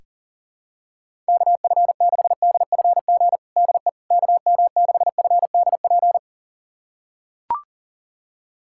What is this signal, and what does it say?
Morse code, heard as a keyed tone, about 37 wpm, 700 Hz. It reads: KF6DFG DE KM6FDP
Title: USA Callsign DE Callsign - 6 Characters - 40wpm